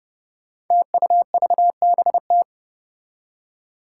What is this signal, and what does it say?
Morse code, heard as a keyed tone, about 30 wpm, 700 Hz. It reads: TUVBT
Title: Five Challenging Letter-Number Combinations 30wpm